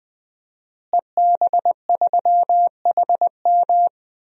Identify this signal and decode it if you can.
Morse code, heard as a keyed tone, about 20 wpm, 700 Hz. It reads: EB3HM
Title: DXCC-100 Call Signs - 5 Characters - Repeated 2x - 20wpm